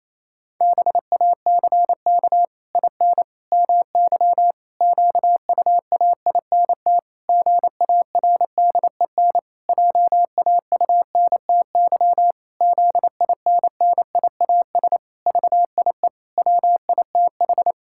Morse code, heard as a keyed tone, about 28 wpm, 700 Hz. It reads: BACK IN MY QUAINT GARDEN JAUNTY ZINNIAS VIE WITH
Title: Pangram Challenge v2 - 28wpm